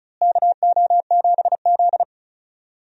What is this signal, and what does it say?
Morse code, heard as a keyed tone, about 34 wpm, 700 Hz. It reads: KO7Z
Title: USA Call Signs - 4 Characters - Repeated 2x - 35wpm